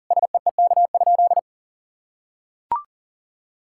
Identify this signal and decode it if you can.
Morse code, heard as a keyed tone, about 40 wpm, 700 Hz. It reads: SEEK?
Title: Sentences from Top 1600 Words 40wpm